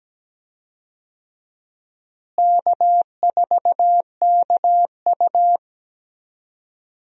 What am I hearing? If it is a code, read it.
Morse code, heard as a keyed tone, about 17 wpm, 700 Hz. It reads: K4KU
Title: USA Call Signs - 4 Characters - Repeated 3x - 17wpm